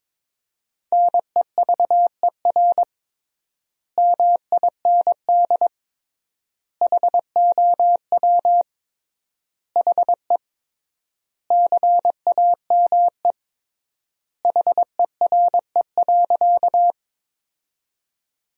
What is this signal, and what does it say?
Morse code, heard as a keyed tone, about 21 wpm, 700 Hz. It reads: NEVER MIND HOW HE CAME HERE.
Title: 3x Word-Spacing - Sentences from Top 500 Words - 22wpm